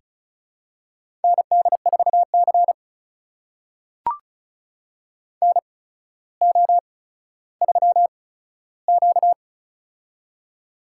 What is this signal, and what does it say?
Morse code, heard as a keyed tone, about 36 wpm, spaced as for 15 wpm, 700 Hz. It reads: ND4C NO3Q
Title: USA Call Signs - 4 Characters - 8x Character Spacing - 35wpm